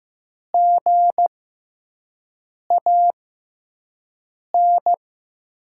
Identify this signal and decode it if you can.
Morse code, heard as a keyed tone, about 15 wpm, spaced as for 5 wpm, 700 Hz. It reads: GAN
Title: USA Call Signs - 5 Characters - 6x Character Spacing - 15wpm